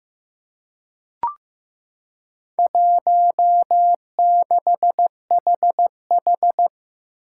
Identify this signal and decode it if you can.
Morse code, heard as a keyed tone, about 15 wpm, 700 Hz. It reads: 16HH
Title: Four Challenging Letter-Number Combinations 15wpm